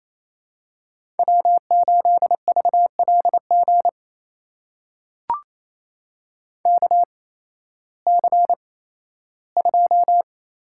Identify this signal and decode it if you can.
Morse code, heard as a keyed tone, about 28 wpm, spaced as for 15 wpm, 700 Hz. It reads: W8VLG KC2
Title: USA Call Signs - 8x Character Spacing - 28wpm